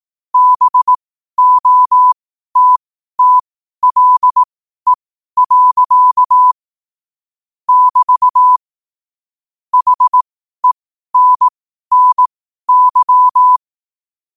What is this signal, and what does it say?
Morse code, heard as a keyed tone, about 18 wpm, spaced as for 12 wpm, 1000 Hz. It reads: BOTTLE. <BT> HENNY
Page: Quotes for Sun, 27 Apr 2025 in Morse Code at 12 words per minute.